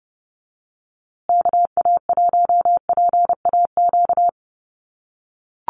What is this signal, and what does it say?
Morse code, heard as a keyed tone, about 30 wpm, 700 Hz. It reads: KA1PAQ
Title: USA Call Signs Repeated 3x 30wpm